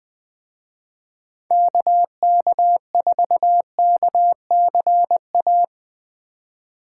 Morse code, heard as a keyed tone, about 22 wpm, 700 Hz. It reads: KK4KCA
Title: Speed Racing - USA Call Signs 20wpm